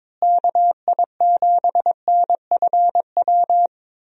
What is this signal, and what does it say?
Morse code, heard as a keyed tone, about 22 wpm, 700 Hz. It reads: KI7NFW
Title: USA Call Signs - 6 Characters - Repeated 2x - 22wpm